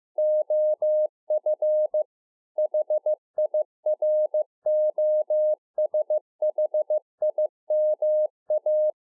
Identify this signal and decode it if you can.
Morse code, heard as a keyed tone, about 15 wpm, 610 Hz. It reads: OF HIROSHIMA